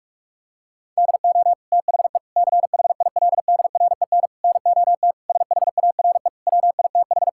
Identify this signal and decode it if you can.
Morse code, heard as a keyed tone, about 46 wpm, 700 Hz. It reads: DO THE CHILDREN NOT SHARE WITH
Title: Sentences from Top 1000 Words - Review Part 6 - 45wpm